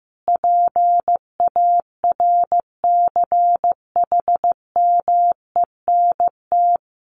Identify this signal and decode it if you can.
Morse code, heard as a keyed tone, about 15 wpm, 700 Hz. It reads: PARCHMENT
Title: A Sound Of Thunder 15wpm